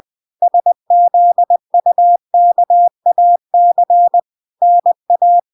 Morse code, heard as a keyed tone, about 20 wpm, 700 Hz. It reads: SZUKAC NA